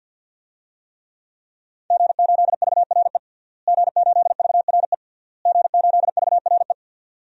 Morse code, heard as a keyed tone, about 50 wpm, 700 Hz. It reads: G8VRE G8VRE G8VRE
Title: DXCC-100 Call Signs - 5 Characters - Repeated 3x - 50wpm